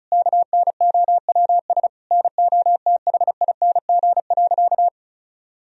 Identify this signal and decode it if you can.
Morse code, heard as a keyed tone, about 36 wpm, 700 Hz. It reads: KNOWS NOTHING.
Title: Sentences from Top 1000 Words - Review Part 1 - 35wpm